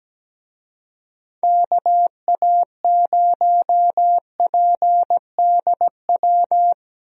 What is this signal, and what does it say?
Morse code, heard as a keyed tone, about 17 wpm, 700 Hz. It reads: KA0PDW
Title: USA Call Signs - 6 Characters - Repeated 3x - 17wpm